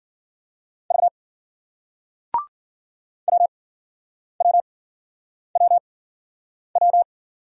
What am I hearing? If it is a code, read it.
Morse code, heard as a keyed tone, about 56 wpm, spaced as for 9 wpm, 700 Hz. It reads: V WWWW